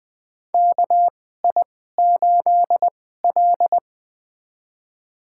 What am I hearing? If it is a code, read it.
Morse code, heard as a keyed tone, about 20 wpm, spaced as for 14 wpm, 700 Hz. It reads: KI8L
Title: USA Call Signs - 4 Characters - 2x Character Spacing - 20wpm